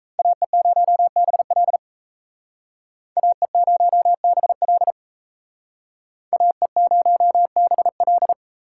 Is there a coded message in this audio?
Yes: Morse code from 0.2 to 8.3 s, heard as a keyed tone, about 37 wpm, 700 Hz. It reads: AE0BL AE0BL AE0BL